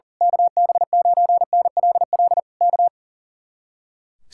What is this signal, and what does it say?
Morse code, heard as a keyed tone, about 35 wpm, 700 Hz. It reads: KB9NLL K
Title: Speed-Racing Common QSO Phrases - 40wpm